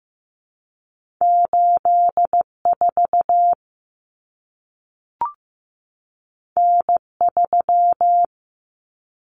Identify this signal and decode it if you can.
Morse code, heard as a keyed tone, about 15 wpm, 700 Hz. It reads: 84 N3
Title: Common Cut Numbers - 2-Digit 15wpm